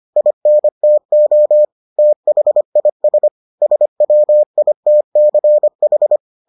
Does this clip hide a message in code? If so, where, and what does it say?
Yes: Morse code from 0.2 to 6.2 s, heard as a keyed tone, about 25 wpm, 600 Hz. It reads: INTO THIS SWITCH